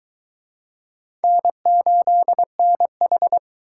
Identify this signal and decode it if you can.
Morse code, heard as a keyed tone, about 23 wpm, 700 Hz. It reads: N8NH